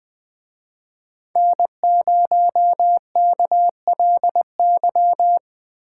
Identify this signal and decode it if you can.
Morse code, heard as a keyed tone, about 24 wpm, 700 Hz. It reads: N0KLY
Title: Speed Racing - USA 1x3 Call Signs 20wpm